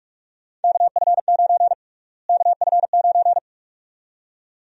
Morse code, heard as a keyed tone, about 46 wpm, 700 Hz. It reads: KF9 KF9
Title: USA Call Sign Prefixes - Repeated 2x - 45wpm